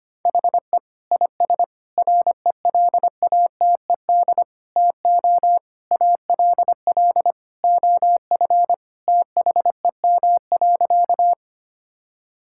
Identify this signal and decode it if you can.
Morse code, heard as a keyed tone, about 25 wpm, 700 Hz. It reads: HE IS RELATED TO ALL OF THEM.
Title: Sentences from Top 1000 Words - Review Part 5 - 25wpm